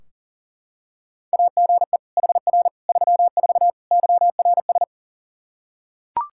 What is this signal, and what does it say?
Morse code, heard as a keyed tone, about 40 wpm, 700 Hz. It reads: AGE HR 34 YRS
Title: K1USN SST QSO Practice 40wpm